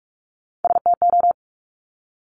Morse code, heard as a keyed tone, about 46 wpm, 700 Hz. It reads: STO
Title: Top Three Letter Combinations in Words 45wpm